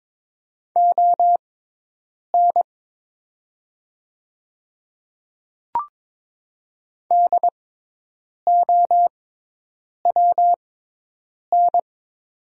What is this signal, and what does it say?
Morse code, heard as a keyed tone, about 22 wpm, spaced as for 8 wpm, 700 Hz. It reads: ON DOWN
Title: 6x Character Spacing - Top 200 Words - 22wpm